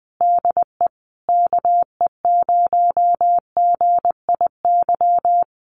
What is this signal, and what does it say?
Morse code, heard as a keyed tone, about 20 wpm, 700 Hz. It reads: DE KE0GIY
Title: Common QSO Phrases Encore 20wpm